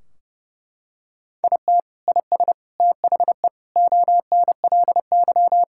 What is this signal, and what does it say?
Morse code, heard as a keyed tone, about 30 wpm, 700 Hz. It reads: IT IS THE ONLY
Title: Sentences from Top 1600 Words 30wpm